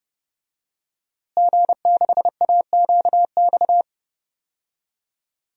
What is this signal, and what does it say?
Morse code, heard as a keyed tone, about 30 wpm, 700 Hz. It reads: G6AQX
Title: DXCC-100 Call Signs - 5 Characters - Repeated 3x - 30wpm